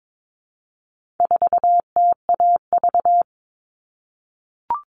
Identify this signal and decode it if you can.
Morse code, heard as a keyed tone, about 22 wpm, 700 Hz. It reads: 4TAV
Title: All Cut Numbers - 4-Digit 22wpm